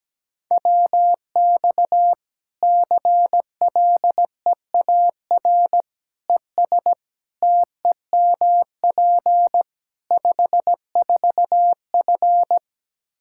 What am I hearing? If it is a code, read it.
Morse code, heard as a keyed tone, about 17 wpm, 700 Hz. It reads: WX CLEAR ES TEMP 54F
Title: Practice QSO - Exchange 2 - Info - 17wpm